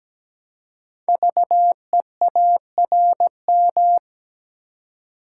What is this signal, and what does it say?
Morse code, heard as a keyed tone, about 17 wpm, 700 Hz. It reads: VEARM
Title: Code Groups - 5 Letters - 17wpm